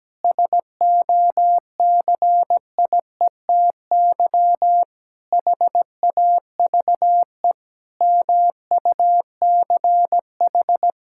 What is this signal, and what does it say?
Morse code, heard as a keyed tone, about 17 wpm, 700 Hz. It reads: SOCIETY HAVE MUCH
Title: Sentences from Top 1000 Words - Review Part 6 - 17wpm